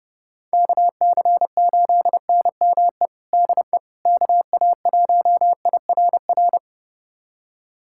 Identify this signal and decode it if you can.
Morse code, heard as a keyed tone, about 29 wpm, 700 Hz. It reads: KC8NME DE KA1IRR
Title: USA Callsign DE Callsign - 6 Characters - 30wpm